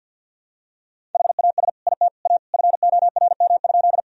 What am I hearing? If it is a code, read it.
Morse code, heard as a keyed tone, about 52 wpm, 700 Hz. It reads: HAS IT A FORM?
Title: Sentences from Top 1000 Words - Review Part 1 - 50wpm